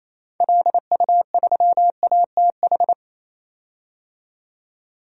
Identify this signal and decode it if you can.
Morse code, heard as a keyed tone, about 28 wpm, 700 Hz. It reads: LU3ATH
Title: DXCC-100 Call Signs - 6 Characters - 28wpm